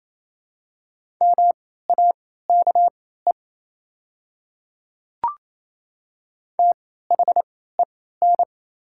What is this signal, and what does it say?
Morse code, heard as a keyed tone, about 28 wpm, spaced as for 16 wpm, 700 Hz. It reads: MAKE THEN
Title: 3x Character Spacing - Top 100 Words - 28wpm